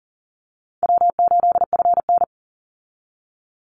Morse code, heard as a keyed tone, about 40 wpm, 700 Hz. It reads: W8FN
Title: Easy Call Signs - 40wpm